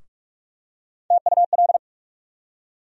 Morse code, heard as a keyed tone, about 42 wpm, 700 Hz. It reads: TUL
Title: USA Call Sign Postfixes - 3 Letters 45wpm